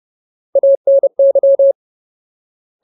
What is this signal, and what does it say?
Morse code, heard as a keyed tone, about 30 wpm, 550 Hz. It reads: ANY
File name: Ultimate Word List / 30WPM Ultimate